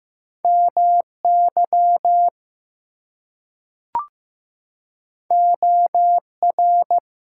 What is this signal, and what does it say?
Morse code, heard as a keyed tone, about 15 wpm, 700 Hz. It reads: MY OR
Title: Two Letter Words - 15wpm